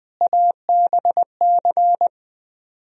Morse code, heard as a keyed tone, about 20 wpm, 700 Hz. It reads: ABC